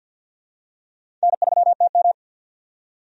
Morse code, heard as a keyed tone, about 50 wpm, 700 Hz. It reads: N3TM
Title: USA Call Signs - 4 Characters - 50wpm